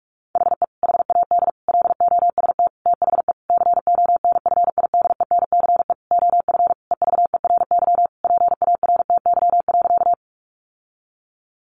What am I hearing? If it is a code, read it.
Morse code, heard as a keyed tone, about 44 wpm, 700 Hz. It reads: HE HAD LOST THE CONFIDENCE OF EVERY PARTY.
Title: Sentences from Top 1500 Words 45wpm